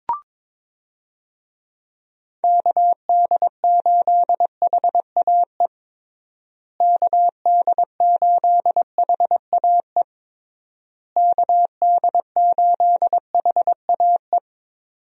Morse code, heard as a keyed tone, about 22 wpm, 700 Hz. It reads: KD8HAE KD8HAE KD8HAE
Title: USA Call Signs - 6 Characters - Repeated 3x - 22wpm